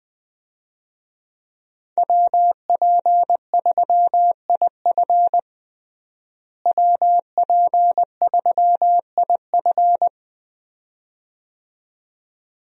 Morse code, heard as a keyed tone, about 21 wpm, 700 Hz. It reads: WP3IF WP3IF
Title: USA Call Signs - 5 Characters - Repeated 2x - 20wpm